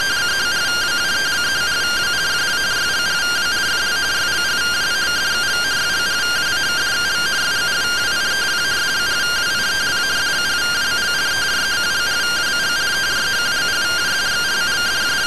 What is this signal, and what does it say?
Two-tone teleprinter signal, about 170 Hz shift.
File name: findings / SAGITTAL 43mm 4mm.gen-10 rtty rx